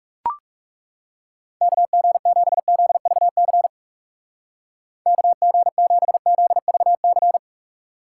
Morse code, heard as a keyed tone, about 35 wpm, 700 Hz. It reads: KG7ZVC KG7ZVC
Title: Speed Racing - USA 2x3 Call Signs 30wpm